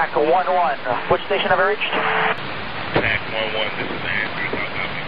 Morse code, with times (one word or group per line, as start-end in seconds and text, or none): none